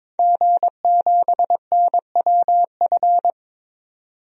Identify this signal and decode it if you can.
Morse code, heard as a keyed tone, about 22 wpm, 700 Hz. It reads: G7NWF
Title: DXCC-100 Call Signs - 5 Characters - 22wpm